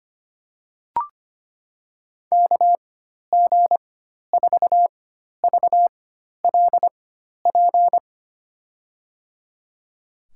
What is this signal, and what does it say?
Morse code, heard as a keyed tone, about 25 wpm, spaced as for 12 wpm, 700 Hz. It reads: KG4VLP
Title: USA Call Signs - 6 Characters - 4x Character Spacing - 25wpm